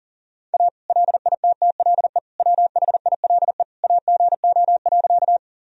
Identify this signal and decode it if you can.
Morse code, heard as a keyed tone, about 40 wpm, 700 Hz. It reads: A LITTLE WHILE AGO.
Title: Sentences from Top 1000 Words - Review Part 3 - 40wpm